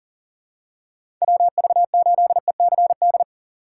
Morse code, heard as a keyed tone, about 40 wpm, 700 Hz. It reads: WV8ECD
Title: USA Call Signs - 2x Character Spacing - 40wpm